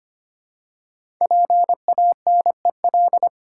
Morse code, heard as a keyed tone, about 25 wpm, 700 Hz. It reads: PANEL
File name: 5 Letter Words - First 3 Letters Given - 25wpm